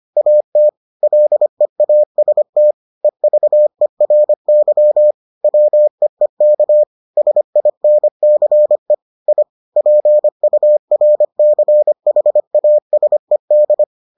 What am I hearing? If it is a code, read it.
Morse code, heard as a keyed tone, about 25 wpm, 600 Hz. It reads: AT LEAST EVERY WEEK SINCE I PURCHASED